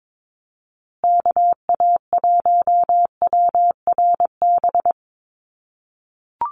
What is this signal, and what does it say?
Morse code, heard as a keyed tone, about 22 wpm, 700 Hz. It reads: KA1WRB
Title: USA Call Signs Repeated 3x 22wpm